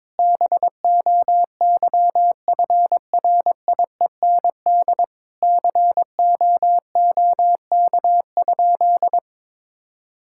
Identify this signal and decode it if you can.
Morse code, heard as a keyed tone, about 22 wpm, 700 Hz. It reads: BOYFRIEND COOK?
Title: Sentences from Top 1000 Words - Review Part 5 - 22wpm